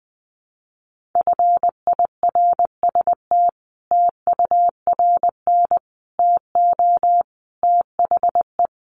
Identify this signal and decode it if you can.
Morse code, heard as a keyed tone, about 19 wpm, 700 Hz. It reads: FIRST TURN TO THE
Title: Sentences from Top 300 Words 20wpm